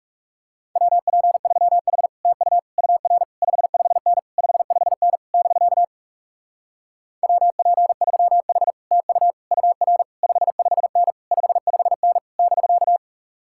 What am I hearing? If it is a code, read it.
Morse code, heard as a keyed tone, about 40 wpm, 700 Hz. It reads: WP3H TU UR 55N 55N <BK> WP3H TU UR 55N 55N <BK>